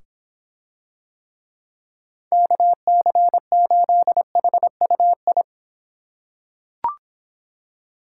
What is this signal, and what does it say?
Morse code, heard as a keyed tone, about 24 wpm, 700 Hz. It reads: KC8HUI